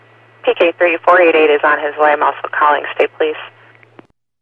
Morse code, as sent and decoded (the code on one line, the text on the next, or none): none